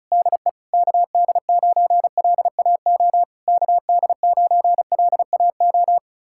Morse code, heard as a keyed tone, about 40 wpm, 700 Hz. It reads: DE KD9LAO KD9LAO